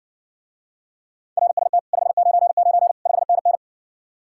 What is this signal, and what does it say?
Morse code, heard as a keyed tone, about 70 wpm, 700 Hz. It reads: RST 599 5NN